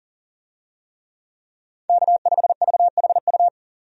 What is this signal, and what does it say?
Morse code, heard as a keyed tone, about 40 wpm, 700 Hz. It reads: K5VHU
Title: USA Call Signs - 5 Characters - 40wpm